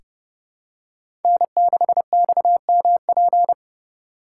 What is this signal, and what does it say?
Morse code, heard as a keyed tone, about 30 wpm, 700 Hz. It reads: N6XMP
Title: USA Call Signs - 3x Character Spacing - 30wpm